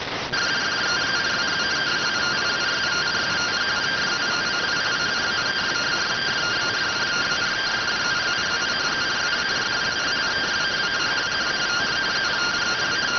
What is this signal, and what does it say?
Two-tone teleprinter signal, about 170 Hz shift.